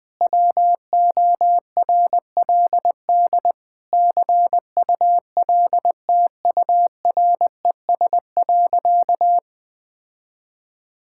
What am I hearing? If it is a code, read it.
Morse code, heard as a keyed tone, about 21 wpm, 700 Hz. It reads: WORLD CULTURES.
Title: Difficult Sentences with Hints - 20wpm